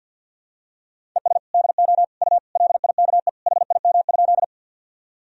Morse code, heard as a keyed tone, about 46 wpm, 700 Hz. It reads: ES DO U LIKE HIM?
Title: Sentences from Top 100 Words and Abbreviations 50wpm